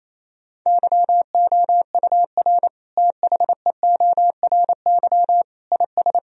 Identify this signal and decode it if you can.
Morse code, heard as a keyed tone, about 28 wpm, 700 Hz. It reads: YOUR THEORY IS
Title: Sentences from Top 1000 Words - Review Part 3 - 28wpm